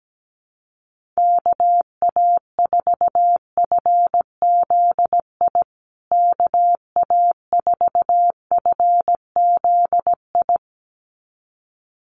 Morse code, heard as a keyed tone, about 17 wpm, 700 Hz. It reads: KA4FZI KA4FZI
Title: USA Call Signs Repeated 2x 17wpm